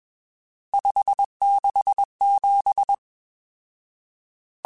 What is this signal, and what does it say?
Morse code, heard as a keyed tone, about 21 wpm, 800 Hz. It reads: 567
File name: RSTs / RST-mp3s / Various RST @20